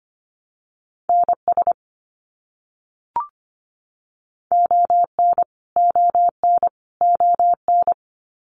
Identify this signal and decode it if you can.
Morse code, heard as a keyed tone, about 25 wpm, 700 Hz. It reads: NS ON ON ON
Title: Canadian Provinces and Territories - Repeated 3x - 25wpm